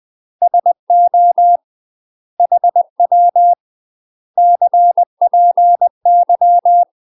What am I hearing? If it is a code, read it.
Morse code, heard as a keyed tone, about 20 wpm, 700 Hz. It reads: SO HW CPY
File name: UK MASTER QSOs TRAINING / 20WPM / UK MASTER QSOs 0037